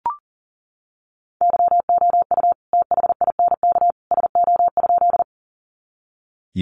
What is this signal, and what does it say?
Morse code, heard as a keyed tone, about 40 wpm, 700 Hz. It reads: YOU THINK SO?
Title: Sentences from Top 200 Words 40wpm